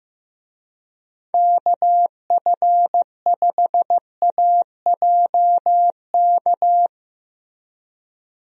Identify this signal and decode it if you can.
Morse code, heard as a keyed tone, about 15 wpm, 700 Hz. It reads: KF5AJK
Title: USA Call Signs - 6 Characters - Repeated 2x - 15wpm